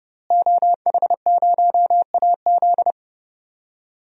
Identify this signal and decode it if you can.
Morse code, heard as a keyed tone, about 30 wpm, 700 Hz. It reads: OH0AZ
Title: DXCC-100 Call Signs - 5 Characters - Repeated 3x - 30wpm